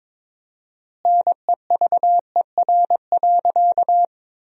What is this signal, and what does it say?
Morse code, heard as a keyed tone, about 22 wpm, 700 Hz. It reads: NEVER.
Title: 4x Word-Spacing - Sentences from Top 300 Words - 22wpm